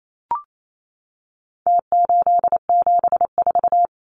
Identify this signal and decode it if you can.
Morse code, heard as a keyed tone, about 28 wpm, 700 Hz. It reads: T874
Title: Numbers - Contest Style - 4-Digit 28wpm